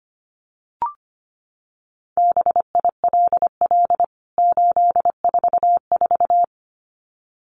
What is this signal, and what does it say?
Morse code, heard as a keyed tone, about 25 wpm, 700 Hz. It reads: BILL 844
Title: CWT Exchange 25wpm